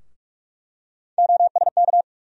Morse code, heard as a keyed tone, about 44 wpm, 700 Hz. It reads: OSK
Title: USA Call Sign Postfixes - 3 Letters 45wpm